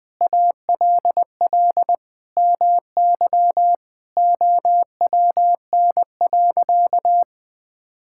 Morse code, heard as a keyed tone, about 21 wpm, 700 Hz. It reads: ALL MY OWN.
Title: Sentences from Top 1000 Words - Review Part 6 - 20wpm